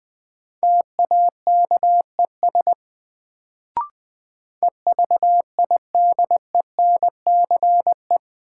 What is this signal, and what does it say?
Morse code, heard as a keyed tone, about 20 wpm, 700 Hz. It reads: TAKES EVIDENCE
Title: Sets of 5 Words Encore - New Word Drill - 20wpm